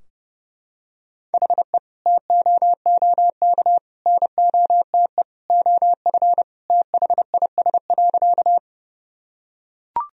Morse code, heard as a keyed tone, about 30 wpm, 700 Hz. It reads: HE TOOK NOTE OF THIS.
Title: Sentences from Top 1000 Words - Review Part 2 - 30wpm